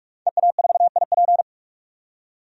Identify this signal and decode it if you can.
Morse code, heard as a keyed tone, about 42 wpm, 700 Hz. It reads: EA4IP
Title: DXCC-100 Call Signs - 5 Characters - 45wpm